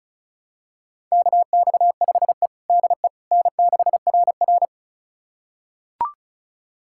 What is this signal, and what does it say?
Morse code, heard as a keyed tone, about 36 wpm, 700 Hz. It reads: KX5E DE N6RR
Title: USA Callsign DE Callsign - 4 Characters - 35wpm